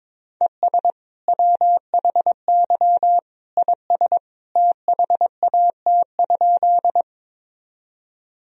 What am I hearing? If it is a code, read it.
Morse code, heard as a keyed tone, about 21 wpm, 700 Hz. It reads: ES WHY IS THAT?
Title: Sentences from Top 200 Words and Abbreviations 22wpm